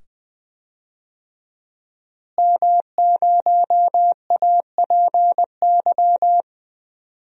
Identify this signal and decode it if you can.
Morse code, heard as a keyed tone, about 20 wpm, 700 Hz. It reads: M0APY
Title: DXCC-100 Call Signs - 5 Characters - Repeated 2x - 20wpm